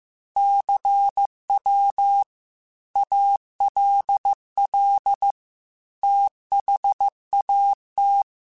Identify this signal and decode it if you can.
Morse code, heard as a keyed tone, about 15 wpm, 800 Hz. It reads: CW ALL THAT
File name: QSOs / QSOs-mp3s / WD0BGP 2@13